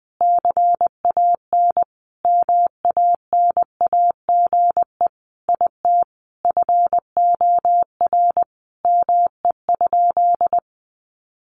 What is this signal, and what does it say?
Morse code, heard as a keyed tone, about 20 wpm, 700 Hz. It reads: CAN MANAGE IT FOR ME?
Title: Sentences from Top 1400 Words 20wpm